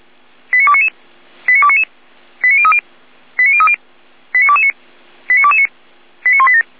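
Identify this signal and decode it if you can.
Five-tone selective call (ZVEI), heard as quick runs of five tones, tones of about 70 ms. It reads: 89299 89299 89030 89030 89229 89229 89187
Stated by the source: ZVEI five-tone selective call 89173, received by SDR.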